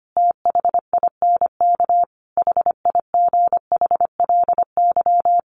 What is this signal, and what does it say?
Morse code, heard as a keyed tone, about 25 wpm, 700 Hz. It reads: THINK HIGHLY
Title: Sentences from Top 1100 Words 25wpm